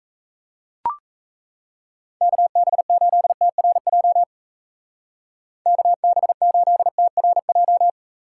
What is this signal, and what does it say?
Morse code, heard as a keyed tone, about 35 wpm, 700 Hz. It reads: KB8TRJ KB8TRJ